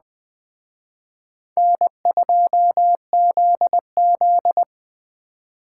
Morse code, heard as a keyed tone, about 20 wpm, 700 Hz. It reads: N2ZZ
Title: Speed Racing - USA 1x2 Call Signs 17wpm